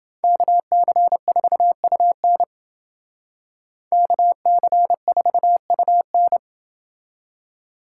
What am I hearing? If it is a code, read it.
Morse code, heard as a keyed tone, about 28 wpm, 700 Hz. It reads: KC4UN KC4UN